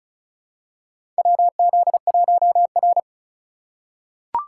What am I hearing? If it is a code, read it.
Morse code, heard as a keyed tone, about 36 wpm, 700 Hz. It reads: WZ1R
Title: USA Call Signs - 3x Character Spacing - 35wpm